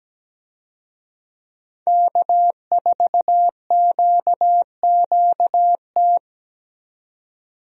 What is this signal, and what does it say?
Morse code, heard as a keyed tone, about 17 wpm, 700 Hz. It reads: K4QQT
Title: USA Call Signs - 5 Characters - 17wpm